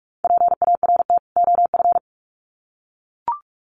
Morse code, heard as a keyed tone, about 44 wpm, 700 Hz. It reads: PART OF
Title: Sets of 2 Words 45wpm